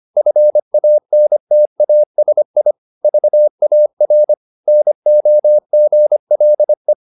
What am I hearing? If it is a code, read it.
Morse code, heard as a keyed tone, about 25 wpm, 600 Hz. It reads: FANTASI VAR NOGLE
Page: Ferie 25 wpm